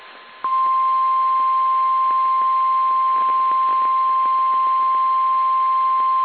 Emergency Alert System (SAME) tones (the 1050 Hz attention tone). It is heard from 1.1 to 6.0 s.
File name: eas / KSDZKDJL / KSDZKDJL